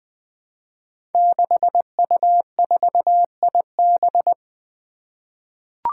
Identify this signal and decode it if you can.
Morse code, heard as a keyed tone, about 20 wpm, 700 Hz. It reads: 6U4IB
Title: Five Challenging Letter-Number Combinations 20wpm